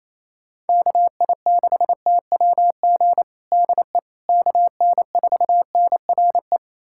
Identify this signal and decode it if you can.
Morse code, heard as a keyed tone, about 28 wpm, 700 Hz. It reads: KI6TWG DE KN4NRE
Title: USA Callsign DE Callsign - 6 Characters - 28wpm